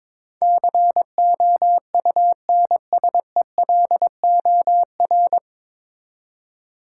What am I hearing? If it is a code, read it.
Morse code, heard as a keyed tone, about 21 wpm, 700 Hz. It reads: COUNSELOR
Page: Morse Code Ninja practice 22wpm